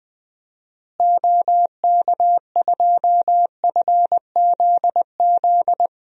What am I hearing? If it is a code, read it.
Morse code, heard as a keyed tone, about 21 wpm, 700 Hz. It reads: OK2FZZ
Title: DXCC-100 Call Signs - 6 Characters - Repeated 3x - 20wpm